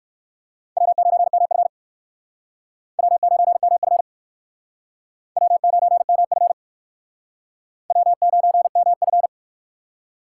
Morse code, heard as a keyed tone, about 56 wpm, 700 Hz. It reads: W9MF W9MF W9MF W9MF
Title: Speed Racing - USA 1x2 Call Signs 45wpm